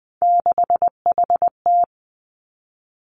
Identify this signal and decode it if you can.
Morse code, heard as a keyed tone, about 20 wpm, 700 Hz. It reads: 6HT
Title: Three Challenging Letter-Number Combinations 20wpm